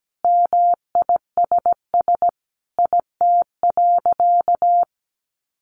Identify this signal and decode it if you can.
Morse code, heard as a keyed tone, about 17 wpm, 700 Hz. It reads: MISS IT.
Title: A Sound Of Thunder 17wpm